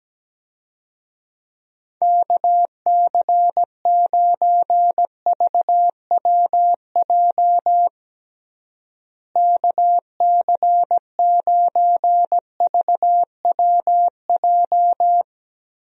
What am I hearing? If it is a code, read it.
Morse code, heard as a keyed tone, about 17 wpm, 700 Hz. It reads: KC9VWJ KC9VWJ
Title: USA Call Signs - 6 Characters - Repeated 3x - 17wpm